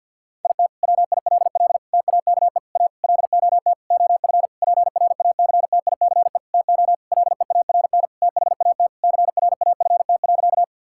Morse code, heard as a keyed tone, about 50 wpm, 700 Hz. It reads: IT WILL TAKE A LOT OF PRACTICE TO LEARN THAT CRAFT.
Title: Sentences from Top 1000 Words - Review Part 6 - 50wpm